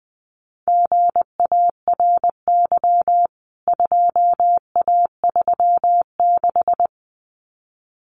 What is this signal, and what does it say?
Morse code, heard as a keyed tone, about 20 wpm, 700 Hz. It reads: GARY 2A36
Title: CWT Exchange 20wpm